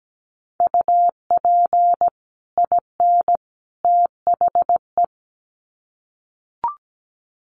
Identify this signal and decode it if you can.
Morse code, heard as a keyed tone, about 17 wpm, 700 Hz. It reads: UP IN THE
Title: Sets of 3 Words 17wpm